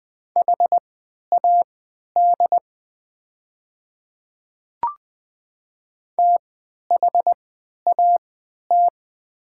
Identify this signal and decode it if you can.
Morse code, heard as a keyed tone, about 20 wpm, spaced as for 11 wpm, 700 Hz. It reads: HAD THAT
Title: 3x Character Spacing - Top 200 Words - 20wpm